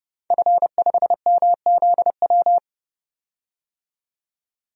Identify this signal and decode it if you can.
Morse code, heard as a keyed tone, about 30 wpm, 700 Hz. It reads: F5MZW
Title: DXCC-100 Call Signs - 5 Characters - 30wpm